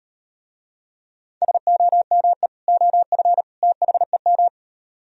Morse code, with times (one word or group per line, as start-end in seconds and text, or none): 1.4-2.5 SOME
2.7-3.4 OF
3.6-4.5 THEM